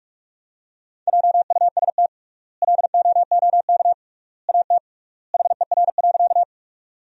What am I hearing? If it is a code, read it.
Morse code, heard as a keyed tone, about 46 wpm, 700 Hz. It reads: JUST LOOK AT HER.
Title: 3x Word-Spacing - Sentences from Top 200 Words - 45wpm